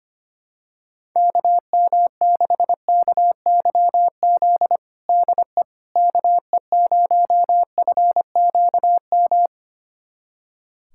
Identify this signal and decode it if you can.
Morse code, heard as a keyed tone, about 25 wpm, 700 Hz. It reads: KM6KYZ DE KE0FQM
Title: USA Callsign DE Callsign 25wpm